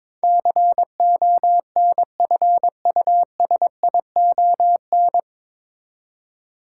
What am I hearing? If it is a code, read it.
Morse code, heard as a keyed tone, about 21 wpm, 700 Hz. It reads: CONFUSION
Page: Morse Code Ninja practice 22wpm